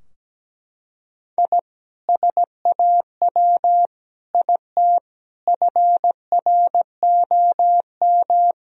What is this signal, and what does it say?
Morse code, heard as a keyed tone, about 17 wpm, 700 Hz. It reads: I SAW IT FROM
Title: Sentences from Top 1000 Words - Review Part 2 - 17wpm